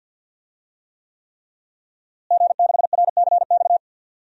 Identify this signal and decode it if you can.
Morse code, heard as a keyed tone, about 47 wpm, 700 Hz. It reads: G6RCX
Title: DXCC-100 Call Signs - 5 Characters - Repeated 3x - 50wpm